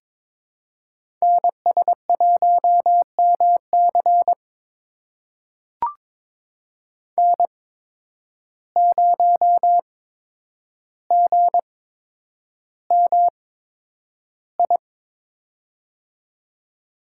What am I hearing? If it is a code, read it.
Morse code, heard as a keyed tone, about 22 wpm, spaced as for 9 wpm, 700 Hz. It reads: NS1MC N0GMI
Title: USA Call Signs - 5 Characters - 8x Character Spacing - 22wpm